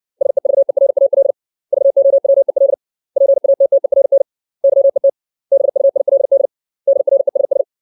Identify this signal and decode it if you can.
Morse code, heard as a keyed tone, about 60 wpm, 550 Hz. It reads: SEVERAL VOWEL PATTERN YET BUILD DRESS
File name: 632 words / 632 Words 60WPM